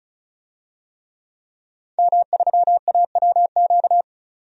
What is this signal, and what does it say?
Morse code, heard as a keyed tone, about 34 wpm, 700 Hz. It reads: M3AWQ
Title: DXCC-100 Call Signs - 5 Characters - 35wpm